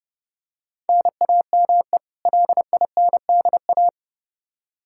Morse code, heard as a keyed tone, about 30 wpm, 700 Hz. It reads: NAME LINDA
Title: Practice QSO - Exchange 1 - Beginning - 30wpm